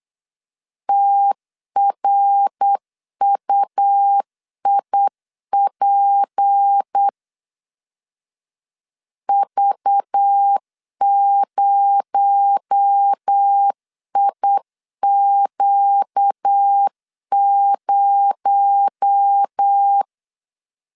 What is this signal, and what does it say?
Morse code, heard as a keyed tone, about 8 wpm, 800 Hz. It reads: TRUIP V0IQ0
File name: CW / DET / 1 6 1500C S C 8wpm 5219